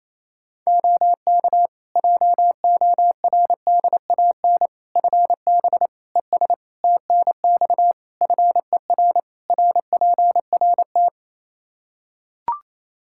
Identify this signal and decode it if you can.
Morse code, heard as a keyed tone, about 28 wpm, 700 Hz. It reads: OK JORDAN FB ES TNX FER RPRT
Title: Practice QSO - Exchange 2 - Info - 28wpm